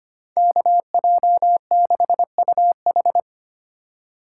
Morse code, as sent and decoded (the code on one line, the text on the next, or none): -.- .--- -.... ..- ....
KJ6UH